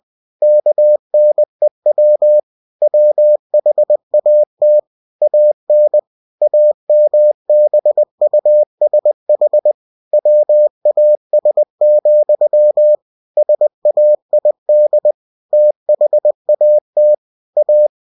Morse code, heard as a keyed tone, about 20 wpm, 600 Hz. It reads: KNEW WHAT AN AMBUSH WAS, SAID THAT A
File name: winnie-20wpm